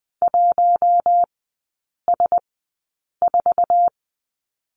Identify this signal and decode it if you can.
Morse code, heard as a keyed tone, about 21 wpm, spaced as for 8 wpm, 700 Hz. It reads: 1S4
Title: 047 Lesson 9e - Number 4 - Three Characters 20wpm